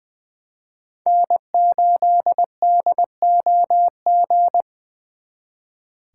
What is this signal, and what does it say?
Morse code, heard as a keyed tone, about 25 wpm, 700 Hz. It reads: N8DOG
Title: Speed Racing - USA Call Signs 20wpm